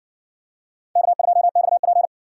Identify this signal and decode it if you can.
Morse code, heard as a keyed tone, about 60 wpm, 700 Hz. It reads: K2XP